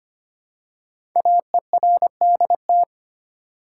Morse code, heard as a keyed tone, about 26 wpm, 700 Hz. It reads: AERDT
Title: Code Groups - 5 Letters - 25wpm